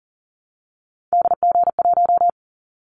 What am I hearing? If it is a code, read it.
Morse code, heard as a keyed tone, about 37 wpm, 700 Hz. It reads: DG1
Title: Callsign Prefixes for Top 50 DXCC - 40wpm